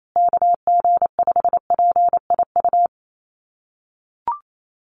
Morse code, heard as a keyed tone, about 28 wpm, 700 Hz. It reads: KG5PIU
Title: USA Call Signs Repeated 3x 28wpm